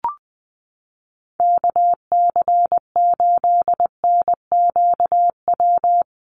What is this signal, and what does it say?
Morse code, heard as a keyed tone, about 20 wpm, 700 Hz. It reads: KC8NQW
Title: USA Call Signs Repeated 3x 20wpm